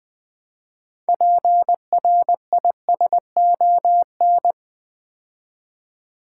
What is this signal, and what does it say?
Morse code, heard as a keyed tone, about 20 wpm, 700 Hz. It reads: PRISON
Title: Build A Word - 3 Letters At A Time - 6-Letter Words - 20wpm